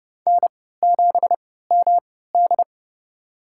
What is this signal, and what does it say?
Morse code, heard as a keyed tone, about 32 wpm, spaced as for 17 wpm, 700 Hz. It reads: N7MD
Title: USA Call Signs - 4 Characters - 3x Character Spacing - 30wpm